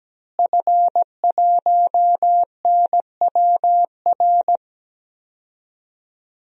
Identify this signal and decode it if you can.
Morse code, heard as a keyed tone, about 17 wpm, 700 Hz. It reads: F1NWR
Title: DXCC-100 Call Signs - 5 Characters - Repeated 2x - 17wpm